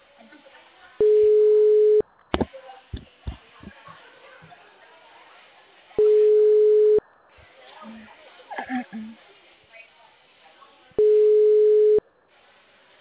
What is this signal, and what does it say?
Ringback tone (Europe, 425 Hz), with speech.